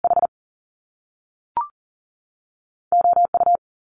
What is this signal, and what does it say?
Morse code, heard as a keyed tone, about 42 wpm, 700 Hz. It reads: H O U
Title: Top Two Letter Combinations in Words 40wpm